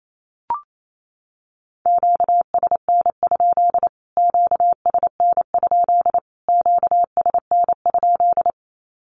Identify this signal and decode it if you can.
Morse code, heard as a keyed tone, about 28 wpm, 700 Hz. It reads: QSN? QSN? QSN?
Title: Q-Codes Repeated 3x 28wpm